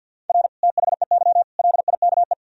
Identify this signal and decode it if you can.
Morse code, heard as a keyed tone, about 51 wpm, 700 Hz. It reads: R THEY LIKE
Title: Sentences from Top 200 Words and Abbreviations 50wpm